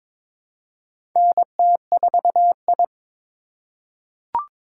Morse code, heard as a keyed tone, about 22 wpm, 700 Hz. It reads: NT4I
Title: USA Call Signs - 4 Characters - 3x Character Spacing - 22wpm